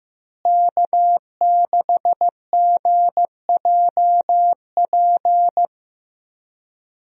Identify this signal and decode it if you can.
Morse code, heard as a keyed tone, about 15 wpm, 700 Hz. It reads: K6GJP
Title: USA Call Signs - 5 Characters - Repeated 2x - 15wpm